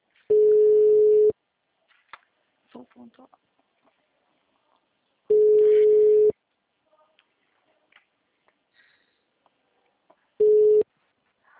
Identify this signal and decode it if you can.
Ringback tone (Europe, 425 Hz), with speech.